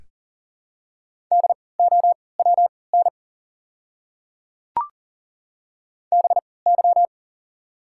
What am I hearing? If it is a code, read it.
Morse code, heard as a keyed tone, about 37 wpm, spaced as for 22 wpm, 700 Hz. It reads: DOWN BY
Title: 3x Character Spacing - Top 100 Words - 40wpm